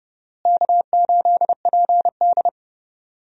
Morse code, heard as a keyed tone, about 31 wpm, 700 Hz. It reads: K8PD
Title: USA Call Signs - 4 Characters - Repeated 3x - 30wpm